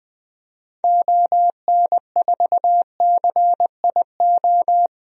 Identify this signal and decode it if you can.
Morse code, heard as a keyed tone, about 20 wpm, 700 Hz. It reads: ON4CIO
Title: DXCC-100 Call Signs - 6 Characters - 20wpm